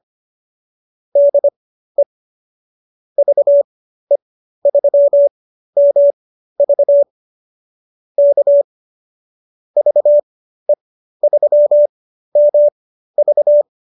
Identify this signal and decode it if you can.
Morse code, heard as a keyed tone, about 25 wpm, spaced as for 13 wpm, 590 Hz. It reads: DE VE3MV K VE3MV